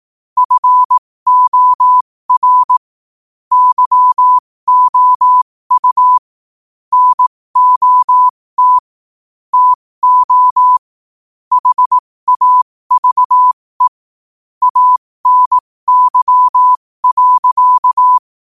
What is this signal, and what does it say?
Morse code, heard as a keyed tone, about 18 wpm, spaced as for 15 wpm, 1000 Hz. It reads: FOR YOU NOT TO HAVE ANY.
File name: qotdfiles / 15wpm